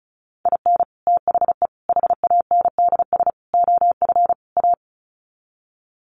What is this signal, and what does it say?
Morse code, heard as a keyed tone, about 34 wpm, 700 Hz. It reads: IN THE HANDS OF A
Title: Sets of 5 Words Encore 35wpm